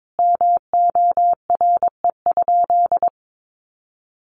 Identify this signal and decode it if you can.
Morse code, heard as a keyed tone, about 22 wpm, 700 Hz. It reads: MORE?
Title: Sentences from Top 1400 Words 22wpm